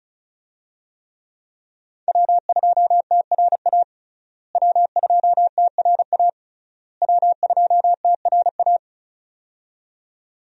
Morse code, heard as a keyed tone, about 36 wpm, 700 Hz. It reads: W2TRA W2TRA W2TRA
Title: USA Call Signs - 5 Characters - Repeated 3x - 35wpm